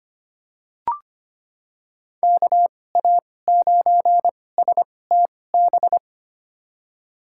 Morse code, heard as a keyed tone, about 25 wpm, spaced as for 18 wpm, 700 Hz. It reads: KA9STB
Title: USA Call Signs - 6 Characters - 2x Character Spacing - 25wpm